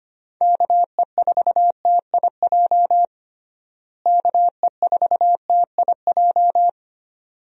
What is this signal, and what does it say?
Morse code, heard as a keyed tone, about 25 wpm, 700 Hz. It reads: KE4TIJ KE4TIJ
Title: USA Call Signs - 6 Characters - Repeated 2x - 25wpm